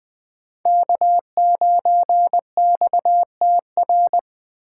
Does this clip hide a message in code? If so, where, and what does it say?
Yes: Morse code from 0.7 to 4.2 s, heard as a keyed tone, about 21 wpm, 700 Hz. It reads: K9XTR